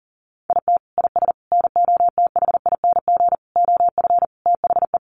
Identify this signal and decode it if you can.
Morse code, heard as a keyed tone, about 40 wpm, 700 Hz. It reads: IT IS NOTHING OF THE
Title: Sentences from Top 1300 Words 40wpm